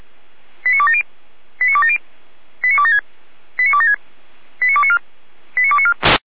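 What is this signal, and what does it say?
Five-tone selective call (ZVEI), heard as quick runs of five tones, tones of about 70 ms. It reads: 89270 89270 89276 89276 89294 89294
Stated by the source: ZVEI five-tone selective call received by SDR, from server of feuerwehr-orlamuende.org (fire brigade Orlamünde).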